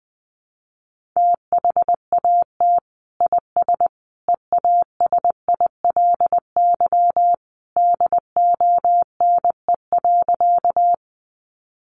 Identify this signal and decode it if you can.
Morse code, heard as a keyed tone, about 20 wpm, 700 Hz. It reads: THAT IS EASILY DONE.
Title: Sentences from Top 900 Words 20wpm